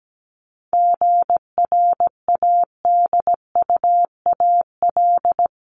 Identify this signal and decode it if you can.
Morse code, heard as a keyed tone, about 17 wpm, 700 Hz. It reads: GRADUAL
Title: LY Suffix - Top 100 Words with Base Words 17wpm